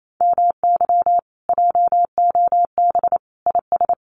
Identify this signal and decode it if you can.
Morse code, heard as a keyed tone, about 28 wpm, 700 Hz. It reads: MY JOB IS
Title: Common QSO Phrases Encore 28wpm